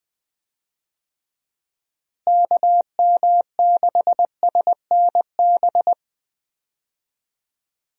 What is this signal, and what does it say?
Morse code, heard as a keyed tone, about 20 wpm, 700 Hz. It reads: KM6SNB
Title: USA Call Signs - 6 Characters - Repeated 2x - 20wpm